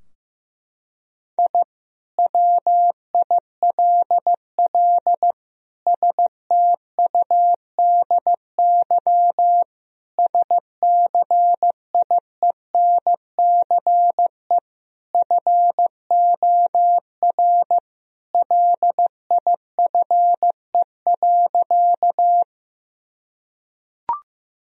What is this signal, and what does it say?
Morse code, heard as a keyed tone, about 15 wpm, 700 Hz. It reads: I WILL STUDY SCIENCE FOR LIFE.
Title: Sentences from Top 1000 Words - Review Part 2 - 15wpm